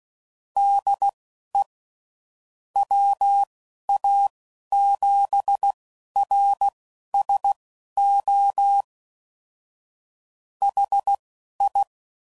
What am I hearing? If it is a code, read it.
Morse code, heard as a keyed tone, about 16 wpm, spaced as for 11 wpm, 800 Hz. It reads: DE WA7RSO HI
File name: CW Code Groups @10WPM / QSOs / QSO WD0FBT